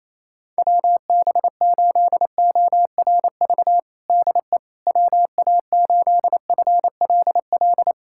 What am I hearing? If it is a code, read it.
Morse code, heard as a keyed tone, about 28 wpm, 700 Hz. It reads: WB8ORV DE WA8FLL
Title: USA Callsign DE Callsign - 6 Characters - 28wpm